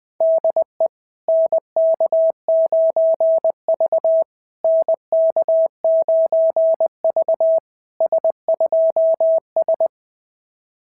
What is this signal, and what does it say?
Morse code, heard as a keyed tone, about 18 wpm, 650 Hz. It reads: DE NK9V NK9V S2S
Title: SOTA QSO Practice 20wpm